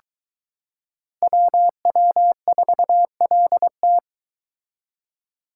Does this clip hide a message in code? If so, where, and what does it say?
Yes: Morse code from 1.2 to 4.0 s, heard as a keyed tone, about 22 wpm, 700 Hz. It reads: WW4LT